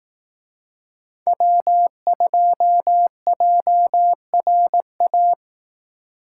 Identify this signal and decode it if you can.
Morse code, heard as a keyed tone, about 18 wpm, 700 Hz. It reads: W2JRA
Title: Speed Racing - USA 1x3 Call Signs 15wpm